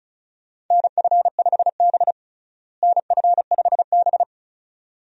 Morse code, heard as a keyed tone, about 36 wpm, 700 Hz. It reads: NF5B NF5B
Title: USA Call Signs - 4 Characters - Repeated 2x - 35wpm